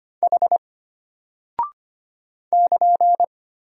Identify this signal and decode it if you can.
Morse code, heard as a keyed tone, about 25 wpm, 700 Hz. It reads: H <KN>
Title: Single Character (Letter:Number:Punctuation:Prosign) Alt 25wpm